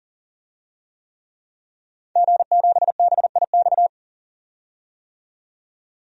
Morse code, heard as a keyed tone, about 40 wpm, 700 Hz. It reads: G7BIX
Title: DXCC-100 Call Signs - 5 Characters - 40wpm